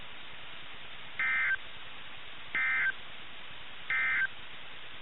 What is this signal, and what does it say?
Emergency Alert System (SAME) tones, heard as harsh buzzing bursts. It is heard from 1.1 to 4.3 s.
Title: NOAA Weather Radio - KIH61